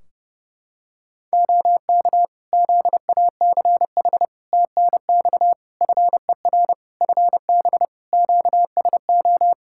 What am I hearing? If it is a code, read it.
Morse code, heard as a keyed tone, about 30 wpm, 700 Hz. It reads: OK ZACH TNX FER FB QSO
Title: K1USN SST QSO Practice 30wpm